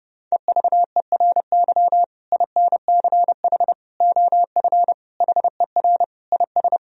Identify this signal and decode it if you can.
Morse code, heard as a keyed tone, about 32 wpm, 700 Hz. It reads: EVERY INCH OF HER IS
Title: Sentences from Top 1000 Words - Review Part 3 - 30wpm